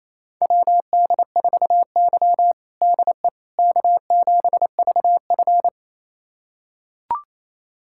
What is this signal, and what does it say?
Morse code, heard as a keyed tone, about 28 wpm, 700 Hz. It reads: WD4Y DE K7VF
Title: USA Callsign DE Callsign - 4 Characters - 28wpm